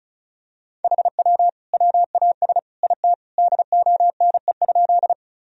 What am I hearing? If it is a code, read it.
Morse code, heard as a keyed tone, about 36 wpm, 700 Hz. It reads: HW WAS IT DONE?
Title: Sentences from Top 500 Words and Abbreviations 35wpm